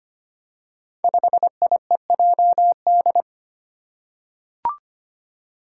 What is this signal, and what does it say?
Morse code, heard as a keyed tone, about 25 wpm, 700 Hz. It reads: 5IEJD
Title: Five Challenging Letter-Number Combinations 25wpm